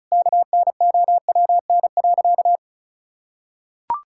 Morse code, heard as a keyed tone, about 36 wpm, 700 Hz. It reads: KNOWN.
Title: Sentences from Top 1000 Words - Review Part 5 - 35wpm